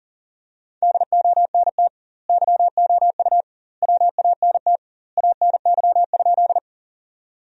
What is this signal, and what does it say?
Morse code, heard as a keyed tone, about 40 wpm, 700 Hz. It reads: DONT YOU WANT ANY?
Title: 2x Word-Spacing - Sentences from Top 300 Words - 40wpm